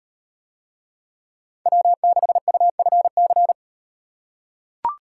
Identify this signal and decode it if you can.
Morse code, heard as a keyed tone, about 38 wpm, 700 Hz. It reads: W6UFC